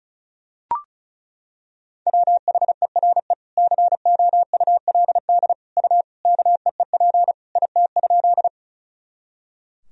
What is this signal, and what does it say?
Morse code, heard as a keyed tone, about 36 wpm, 700 Hz. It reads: WHERE COULD U KEEP IT?
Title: Sentences from Top 300 Words and Abbreviations 35wpm